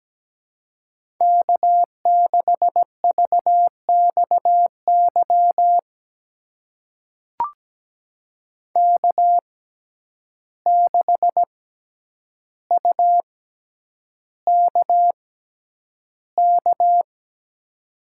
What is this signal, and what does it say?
Morse code, heard as a keyed tone, about 17 wpm, spaced as for 9 wpm, 700 Hz. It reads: K6VXY K6UKK
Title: USA Call Signs - 5 Characters - 6x Character Spacing - 17wpm